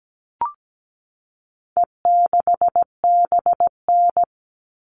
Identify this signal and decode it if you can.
Morse code, heard as a keyed tone, about 17 wpm, 700 Hz. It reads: E6BN
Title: All Cut Numbers - 4-Digit 17wpm